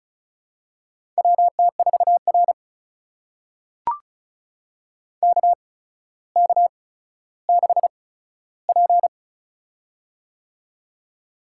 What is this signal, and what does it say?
Morse code, heard as a keyed tone, about 36 wpm, spaced as for 15 wpm, 700 Hz. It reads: WT4R KK6P
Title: USA Call Signs - 4 Characters - 8x Character Spacing - 35wpm